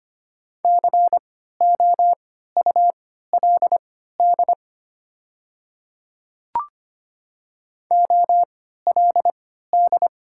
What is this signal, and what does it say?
Morse code, heard as a keyed tone, about 25 wpm, spaced as for 14 wpm, 690 Hz. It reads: COULD OLD
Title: 3x Character Spacing - Top 200 Words - 25wpm